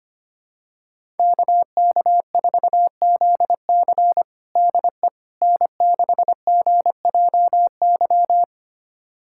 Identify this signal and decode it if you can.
Morse code, heard as a keyed tone, about 25 wpm, 700 Hz. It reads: KK4ZC DE N6GJY
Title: USA Callsign DE Callsign - 5 Characters - 25wpm